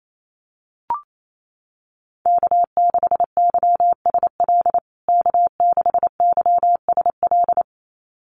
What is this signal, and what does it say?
Morse code, heard as a keyed tone, about 28 wpm, 700 Hz. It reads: K6YSL K6YSL
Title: USA Call Signs Repeated 2x 28wpm